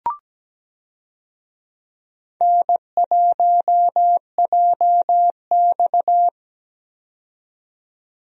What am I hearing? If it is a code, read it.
Morse code, heard as a keyed tone, about 17 wpm, 700 Hz. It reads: N1JX
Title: USA Call Signs - 4 Characters - 17wpm